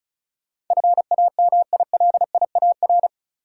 Morse code, heard as a keyed tone, about 34 wpm, 700 Hz. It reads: FAMILIAR
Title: Sentences from Top 1000 Words - New Word Drill - 35wpm